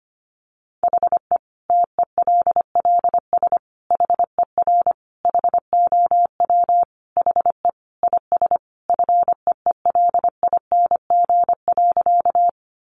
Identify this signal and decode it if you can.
Morse code, heard as a keyed tone, about 25 wpm, 700 Hz. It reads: HE TELLS HER HOW HE IS FEELING.
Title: Additional Easy Phrases 25wpm